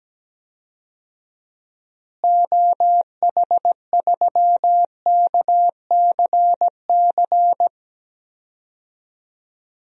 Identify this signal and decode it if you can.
Morse code, heard as a keyed tone, about 17 wpm, 700 Hz. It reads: OH3KCC
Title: DXCC-100 Call Signs - 6 Characters - Repeated 3x - 17wpm